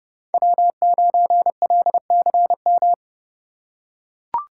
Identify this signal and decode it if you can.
Morse code, heard as a keyed tone, about 30 wpm, 700 Hz. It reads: W9LCM
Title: USA Call Signs - 4x Character Spacing - 30wpm